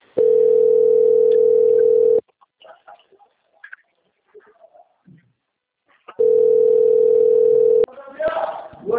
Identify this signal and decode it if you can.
Ringback tone (North America), 440 + 480 Hz, with speech.
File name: out- - - - - .15442